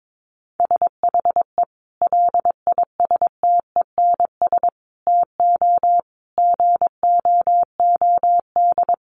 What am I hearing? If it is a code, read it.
Morse code, heard as a keyed tone, about 22 wpm, 700 Hz. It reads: SHE LISTENS TO GOOD
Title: Easy Phrases 22wpm